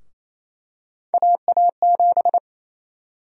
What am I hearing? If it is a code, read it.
Morse code, heard as a keyed tone, about 27 wpm, 700 Hz. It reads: AA7
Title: USA Call Sign Prefixes - Repeated 2x - 28wpm